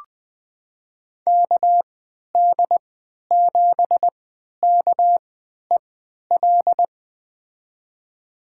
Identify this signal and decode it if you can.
Morse code, heard as a keyed tone, about 20 wpm, spaced as for 11 wpm, 700 Hz. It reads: KD7KEL
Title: USA Call Signs - 3x Character Spacing - 20wpm